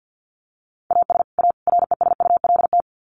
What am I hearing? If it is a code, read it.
Morse code, heard as a keyed tone, about 50 wpm, 700 Hz. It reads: AS A RESULT
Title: Sets of 3 Words 50wpm